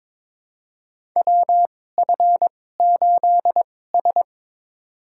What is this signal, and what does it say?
Morse code, heard as a keyed tone, about 22 wpm, spaced as for 16 wpm, 700 Hz. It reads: WF8S
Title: USA Call Signs - 4 Characters - 2x Character Spacing - 22wpm